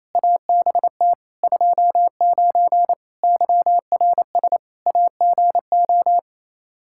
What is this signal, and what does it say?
Morse code, heard as a keyed tone, about 28 wpm, 700 Hz. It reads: ABT 29 YRS AGO
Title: Practice QSO - Exchange 3 - More Info - 28wpm